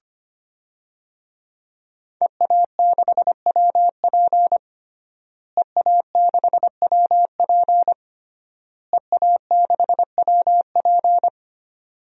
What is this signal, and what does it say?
Morse code, heard as a keyed tone, about 25 wpm, 700 Hz. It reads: EA6WP EA6WP EA6WP
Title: DXCC-100 Call Signs - 5 Characters - Repeated 3x - 25wpm